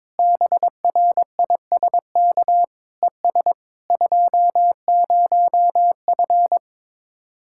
Morse code, heard as a keyed tone, about 22 wpm, 700 Hz. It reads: BRISK ES 20F
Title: Speed-Racing Common QSO Phrases Encore - 22wpm